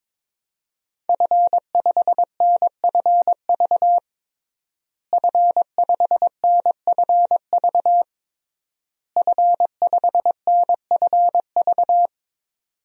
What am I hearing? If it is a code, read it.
Morse code, heard as a keyed tone, about 22 wpm, 700 Hz. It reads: F5NFV F5NFV F5NFV
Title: DXCC-100 Call Signs - 5 Characters - Repeated 3x - 22wpm